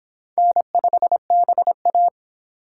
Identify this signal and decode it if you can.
Morse code, heard as a keyed tone, about 29 wpm, 700 Hz. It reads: N5BA